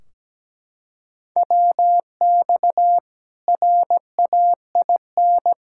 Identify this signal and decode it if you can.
Morse code, heard as a keyed tone, about 17 wpm, 700 Hz. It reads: WX RAIN
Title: Practice QSO - Exchange 2 - Info - 17wpm